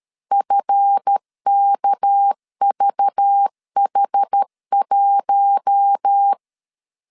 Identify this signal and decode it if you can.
Morse code, heard as a keyed tone, about 13 wpm, 800 Hz. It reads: FKVH1
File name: LEZIONI / 7 GF9Z / 7 GF9 12wpm 2319